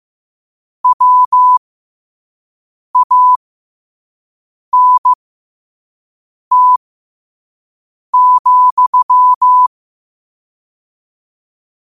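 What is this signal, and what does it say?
Morse code, heard as a keyed tone, about 15 wpm, spaced as for 5 wpm, 1000 Hz. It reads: WANT,
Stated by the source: Morse code sent at 5 wpm.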